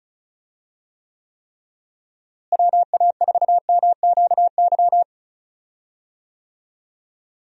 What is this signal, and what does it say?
Morse code, heard as a keyed tone, about 34 wpm, 700 Hz. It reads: WA4MQY
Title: USA Call Signs - 6 Characters - Repeated 3x - 35wpm